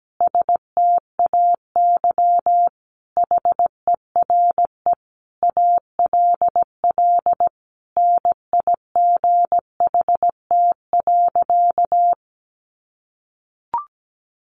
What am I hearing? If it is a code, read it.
Morse code, heard as a keyed tone, about 17 wpm, 700 Hz. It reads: STAY HERE ALL NIGHT.
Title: Sentences from Top 500 Words 17wpm